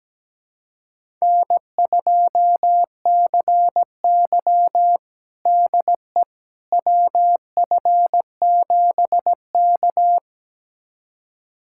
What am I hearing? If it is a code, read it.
Morse code, heard as a keyed tone, about 17 wpm, 700 Hz. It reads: N2CY DE WF7K
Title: USA Callsign DE Callsign - 4 Characters - 17wpm